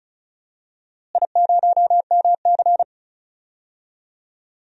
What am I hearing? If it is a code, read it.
Morse code, heard as a keyed tone, about 36 wpm, 700 Hz. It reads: I0MC
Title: DXCC-100 Call Signs - 4 Characters - Repeated 3x - 35wpm